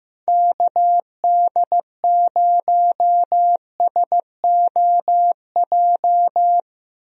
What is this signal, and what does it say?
Morse code, heard as a keyed tone, about 15 wpm, 700 Hz. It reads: KD0SOJ
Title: USA Call Signs - 6 Characters - Repeated 2x - 15wpm